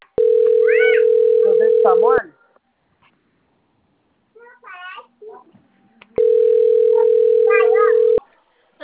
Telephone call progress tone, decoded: ringback tone

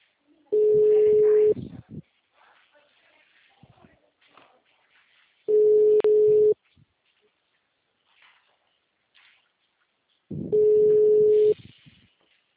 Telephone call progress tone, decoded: ringback tone

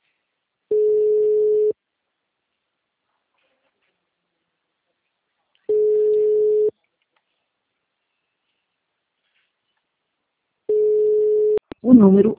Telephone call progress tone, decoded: ringback tone

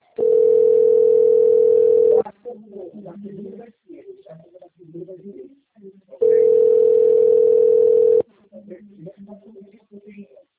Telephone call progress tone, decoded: ringback tone

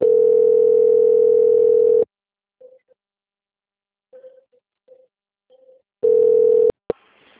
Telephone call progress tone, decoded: ringback tone

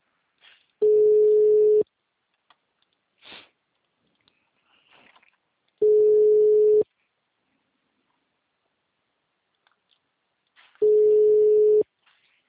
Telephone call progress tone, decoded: ringback tone

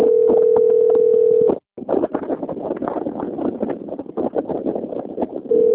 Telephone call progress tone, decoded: ringback tone

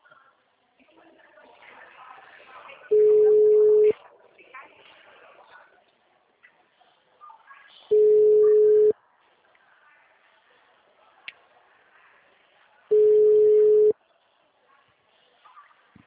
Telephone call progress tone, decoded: ringback tone

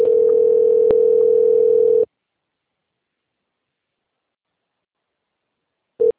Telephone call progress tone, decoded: ringback tone

